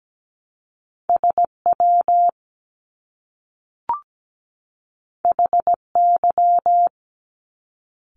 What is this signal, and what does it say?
Morse code, heard as a keyed tone, about 17 wpm, 700 Hz. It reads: SW HY
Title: Top Two Letter Combinations in Words 17wpm